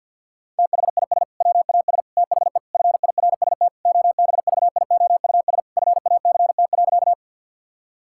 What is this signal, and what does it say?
Morse code, heard as a keyed tone, about 50 wpm, 700 Hz. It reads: THIS WAS THE FIRST OBVIOUS FACT.
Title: Sentences from Top 1600 Words 50wpm